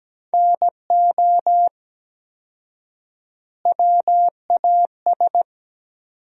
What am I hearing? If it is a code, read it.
Morse code, heard as a keyed tone, about 17 wpm, 700 Hz. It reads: NO WAS
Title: 4x Word-Spacing - Sentences from Top 100 Words - 17wpm